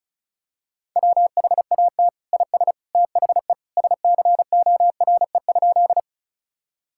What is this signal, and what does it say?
Morse code, heard as a keyed tone, about 34 wpm, 700 Hz. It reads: WHAT IS THE SCORE?
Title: Sentences from Top 1600 Words 35wpm